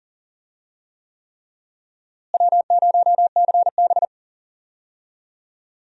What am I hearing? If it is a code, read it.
Morse code, heard as a keyed tone, about 40 wpm, 700 Hz. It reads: W0CB
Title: USA Call Signs - 4 Characters - Repeated 3x - 40wpm